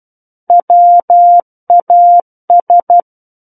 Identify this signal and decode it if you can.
Morse code, heard as a keyed tone, about 12 wpm, 700 Hz. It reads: WAS
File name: Book One / 03-Chapter 3 - 12 WPM